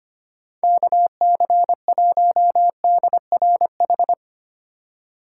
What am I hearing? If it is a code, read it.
Morse code, heard as a keyed tone, about 25 wpm, 700 Hz. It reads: KC1DRH
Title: USA Call Signs - 6 Characters - 2x Character Spacing - 25wpm